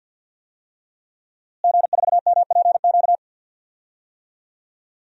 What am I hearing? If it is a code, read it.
Morse code, heard as a keyed tone, about 53 wpm, 700 Hz. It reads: G4MPQ
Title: DXCC-100 Call Signs - 5 Characters - Repeated 3x - 50wpm